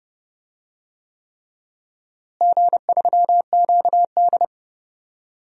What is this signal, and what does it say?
Morse code, heard as a keyed tone, about 30 wpm, 700 Hz. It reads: G3QD
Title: DXCC-100 Call Signs - 4 Characters - Repeated 3x - 30wpm